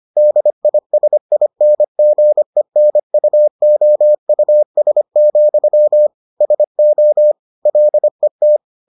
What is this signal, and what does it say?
Morse code, heard as a keyed tone, about 25 wpm, 600 Hz. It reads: DISINGENUOUS, SO LET